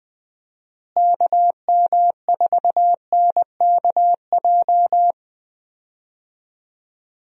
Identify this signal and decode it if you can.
Morse code, heard as a keyed tone, about 20 wpm, 700 Hz. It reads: KM4NKJ
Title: USA Call Signs - 6 Characters - Repeated 2x - 20wpm